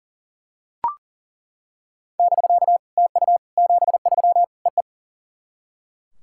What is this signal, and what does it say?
Morse code, heard as a keyed tone, about 39 wpm, 690 Hz. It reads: <BK> TU 73 EE
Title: SOTA QSO Practice 40wpm